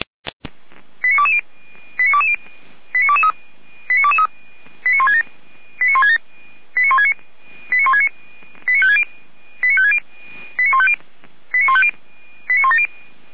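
Five-tone selective call (ZVEI), heard as quick runs of five tones, tones of about 70 ms. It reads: 89220 89220 89223 89167 89167 89169 89169 89560 89560 89150 89150 89170
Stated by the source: ZVEI five-tone call received on SDR.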